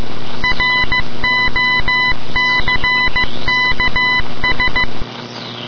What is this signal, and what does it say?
Morse code, heard as a keyed tone, about 15 wpm, 2000 Hz. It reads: ROCKS
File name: dod / ambience / radio morse